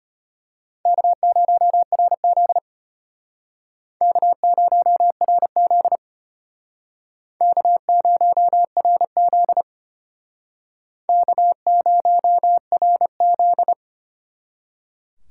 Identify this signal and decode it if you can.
Morse code, heard as a keyed tone, about 31 wpm, 700 Hz. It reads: K0RZ K0RZ K0RZ K0RZ